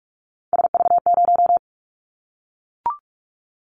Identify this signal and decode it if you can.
Morse code, heard as a keyed tone, about 44 wpm, 700 Hz. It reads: SV0
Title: Callsign Prefixes for Top 50 DXCC - 45wpm